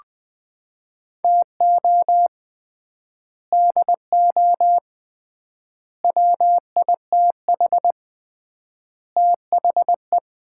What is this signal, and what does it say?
Morse code, heard as a keyed tone, about 20 wpm, 700 Hz. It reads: TO DO WITH THE
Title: 3x Word-Spacing - Sets of 4 Words - 20wpm